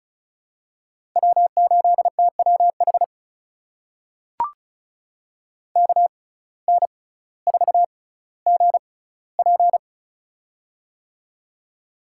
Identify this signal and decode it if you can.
Morse code, heard as a keyed tone, about 36 wpm, spaced as for 18 wpm, 700 Hz. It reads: W8TWH KN4GP
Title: USA Call Signs - 5 Characters - 6x Character Spacing - 35wpm